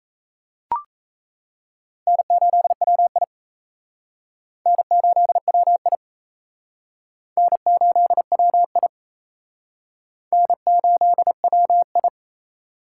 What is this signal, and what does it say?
Morse code, heard as a keyed tone, about 36 wpm, 700 Hz. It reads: N8WI N8WI N8WI N8WI